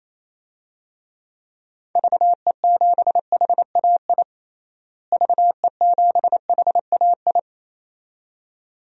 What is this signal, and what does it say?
Morse code, heard as a keyed tone, about 28 wpm, 700 Hz. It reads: VE7HAI VE7HAI
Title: DXCC-100 Call Signs - 6 Characters - Repeated 2x - 28wpm